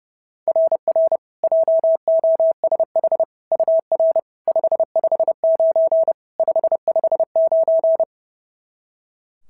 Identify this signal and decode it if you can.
Morse code, heard as a keyed tone, about 30 wpm, 690 Hz. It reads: RR JOSH UR 559 559
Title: SOTA QSO Practice 30wpm